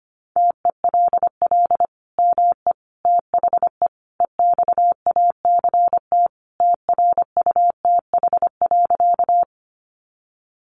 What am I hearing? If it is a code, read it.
Morse code, heard as a keyed tone, about 25 wpm, 700 Hz. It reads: TELL ME THE EXACT TRUTH.
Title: Sentences from Top 1300 Words 25wpm